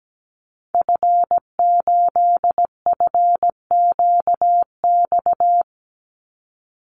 Morse code, heard as a keyed tone, about 17 wpm, 700 Hz. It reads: F8FQX
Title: Call Signs Repeated 3x - 17wpm